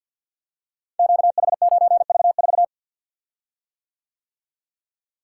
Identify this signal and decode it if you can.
Morse code, heard as a keyed tone, about 46 wpm, 700 Hz. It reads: QH9V4
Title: Code Groups - 5 Letter:Numbers - 50wpm